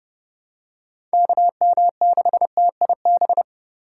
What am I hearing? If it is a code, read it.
Morse code, heard as a keyed tone, about 30 wpm, 700 Hz. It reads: KM6TIB